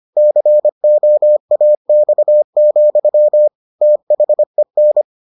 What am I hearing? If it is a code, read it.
Morse code, heard as a keyed tone, about 25 wpm, 600 Hz. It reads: COAX, THEN